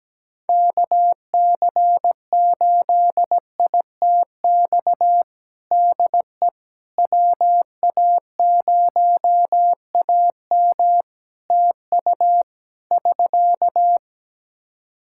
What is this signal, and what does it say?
Morse code, heard as a keyed tone, about 17 wpm, 700 Hz. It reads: KC8ITX DE WA0AM TU <SK>
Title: Practice QSO - Exchange 4 - Ending - 17wpm